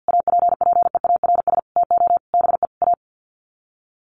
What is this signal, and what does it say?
Morse code, heard as a keyed tone, about 50 wpm, 700 Hz. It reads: APPEARS TO BE A
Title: Sets of 5 Words Encore2 50wpm